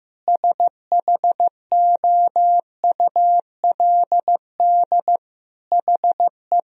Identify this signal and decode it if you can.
Morse code, heard as a keyed tone, about 15 wpm, 700 Hz. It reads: SHOULD HE
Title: Sentences from Top 1000 Words - Review Part 2 - 15wpm